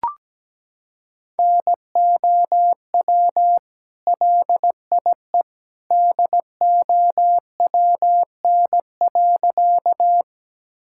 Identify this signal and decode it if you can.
Morse code, heard as a keyed tone, about 17 wpm, 700 Hz. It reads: NOW LIE DOWN.
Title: Sentences from Top 1600 Words 17wpm